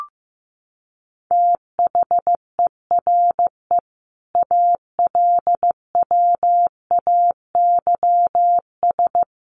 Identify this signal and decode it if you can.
Morse code, heard as a keyed tone, about 14 wpm, 700 Hz. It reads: THERE ALWAYS
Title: Sentences from Top 1100 Words 15wpm